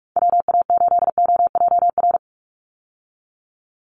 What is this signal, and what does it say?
Morse code, heard as a keyed tone, about 44 wpm, 700 Hz. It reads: WA8OJR
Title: Call Signs Repeated 2x - 45wpm